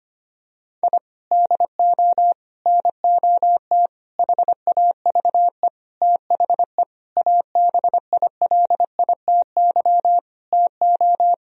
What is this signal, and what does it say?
Morse code, heard as a keyed tone, about 25 wpm, 700 Hz. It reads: I DO NOT HAVE THE ABILITY TO
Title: Sentences from Top 1000 Words - Review Part 3 - 25wpm